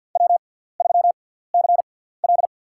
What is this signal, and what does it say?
Morse code, heard as a keyed tone, about 51 wpm, spaced as for 17 wpm, 700 Hz. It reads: W3CL
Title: USA Call Signs - 4 Characters - 6x Character Spacing - 50wpm